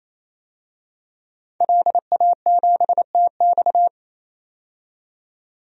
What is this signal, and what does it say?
Morse code, heard as a keyed tone, about 28 wpm, 700 Hz. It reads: LA7TX
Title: DXCC-100 Call Signs - 5 Characters - 28wpm